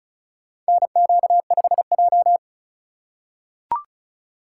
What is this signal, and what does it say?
Morse code, heard as a keyed tone, about 36 wpm, 700 Hz. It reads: NQ5J
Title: USA Call Signs - 4 Characters - 4x Character Spacing - 35wpm